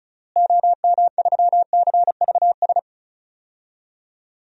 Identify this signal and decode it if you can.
Morse code, heard as a keyed tone, about 36 wpm, 700 Hz. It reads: OM3CVS
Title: DXCC-100 Call Signs - 6 Characters - Repeated 3x - 35wpm